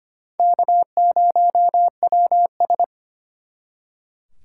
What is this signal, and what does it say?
Morse code, heard as a keyed tone, about 32 wpm, 700 Hz. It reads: K0WS